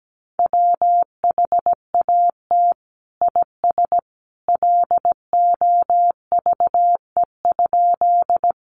Morse code, heard as a keyed tone, about 17 wpm, 700 Hz. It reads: WHAT IS LOVE?
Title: Sentences from Top 400 Words 17wpm